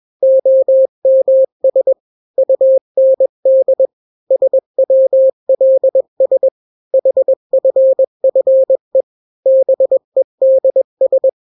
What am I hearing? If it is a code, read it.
Morse code, heard as a keyed tone, about 21 wpm, 530 Hz. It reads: OMS UND SWLS HFFE BEDS
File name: daily / 20wpm